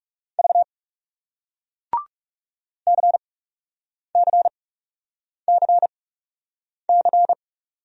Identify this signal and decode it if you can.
Morse code, heard as a keyed tone, about 40 wpm, spaced as for 8 wpm, 700 Hz. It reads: V CCCC